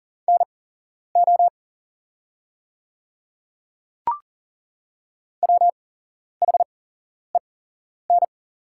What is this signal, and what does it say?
Morse code, heard as a keyed tone, about 40 wpm, spaced as for 11 wpm, 700 Hz. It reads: NO WHEN
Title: 8x Character Spacing - Top 100 Words - 40wpm